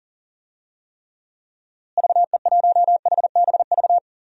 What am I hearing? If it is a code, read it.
Morse code, heard as a keyed tone, about 40 wpm, 700 Hz. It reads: VE1HBV
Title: DXCC-100 Call Signs - 6 Characters - Repeated 3x - 40wpm